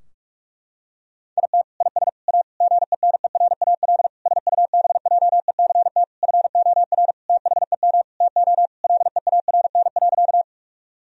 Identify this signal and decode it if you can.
Morse code, heard as a keyed tone, about 46 wpm, 700 Hz. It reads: IT IS A GENERAL SUBJECT FOR THEM TO LEARN.
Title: Sentences from Top 1000 Words - Review Part 3 - 45wpm